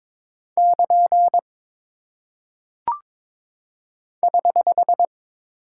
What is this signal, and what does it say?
Morse code, heard as a keyed tone, about 22 wpm, 700 Hz. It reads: <KN> <HH>
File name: practice / 22wpm / Single Punctuation-Prosign - 22wpm